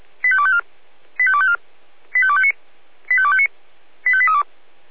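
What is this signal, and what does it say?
Five-tone selective call (ZVEI), heard as quick runs of five tones, tones of about 70 ms. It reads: 85254 85254 85259 85259 85832
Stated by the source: ZVEI five-tone call 85888 received by SDR.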